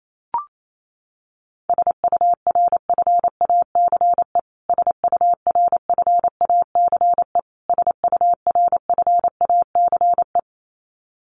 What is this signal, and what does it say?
Morse code, heard as a keyed tone, about 28 wpm, 700 Hz. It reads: SURFACE SURFACE SURFACE
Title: Top 500 Words Repeated 28wpm